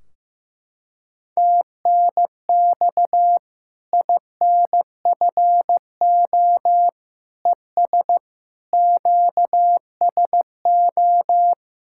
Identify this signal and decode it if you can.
Morse code, heard as a keyed tone, about 15 wpm, 700 Hz. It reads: TNX INFO ES QSO
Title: Practice QSO - Exchange 4 - Ending - 15wpm